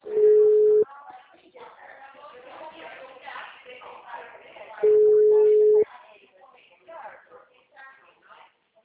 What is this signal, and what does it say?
Ringback tone (Europe, 425 Hz), with speech.
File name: out- -2011- - - .55608